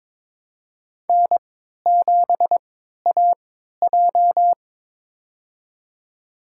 Morse code, heard as a keyed tone, about 22 wpm, spaced as for 13 wpm, 700 Hz. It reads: N7AJ
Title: USA Call Signs - 3x Character Spacing - 22wpm